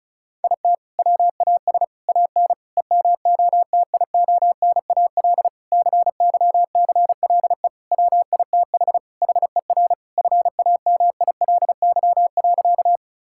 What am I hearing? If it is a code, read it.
Morse code, heard as a keyed tone, about 36 wpm, 690 Hz. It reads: IT WAS AN EMOTIONAL CYCLE WITH HER FAMILY.
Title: Sentences from Top 1000 Words - Review Part 6 - 35wpm